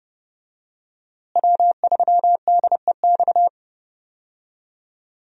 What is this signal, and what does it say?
Morse code, heard as a keyed tone, about 30 wpm, 700 Hz. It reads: W3DEX
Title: USA Call Signs - 5 Characters - Repeated 2x - 30wpm